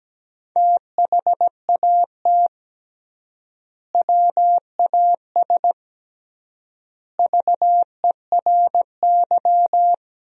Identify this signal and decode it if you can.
Morse code, heard as a keyed tone, about 17 wpm, 700 Hz. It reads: THAT WAS VERY
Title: 3x Word-Spacing - Sentences from Top 200 Words - 17wpm